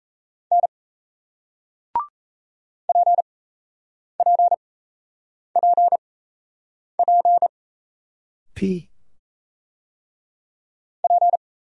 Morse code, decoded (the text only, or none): N PPPP P